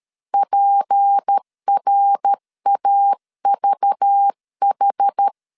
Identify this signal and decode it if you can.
Morse code, heard as a keyed tone, about 13 wpm, 800 Hz. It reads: PRAVH
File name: LEZIONI / 7 GF9Z / 7 GF 12wpm 2122